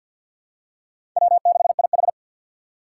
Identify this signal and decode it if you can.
Morse code, heard as a keyed tone, about 49 wpm, 700 Hz. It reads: W6IH
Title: USA Call Signs - 4 Characters - 3x Character Spacing - 50wpm